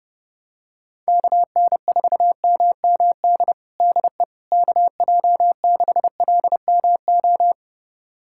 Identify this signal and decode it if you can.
Morse code, heard as a keyed tone, about 30 wpm, 700 Hz. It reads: KN4MMD DE KJ6LMO
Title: USA Callsign DE Callsign - 6 Characters - 30wpm